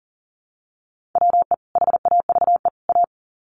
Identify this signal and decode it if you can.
Morse code, heard as a keyed tone, about 40 wpm, 700 Hz. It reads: WE HAVE A
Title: Sets of 3 Words 40wpm